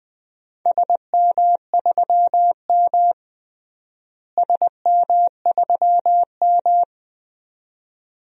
Morse code, heard as a keyed tone, about 21 wpm, 700 Hz. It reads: SM3M SM3M
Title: DXCC-100 Call Signs - 4 Characters - Repeated 2x - 20wpm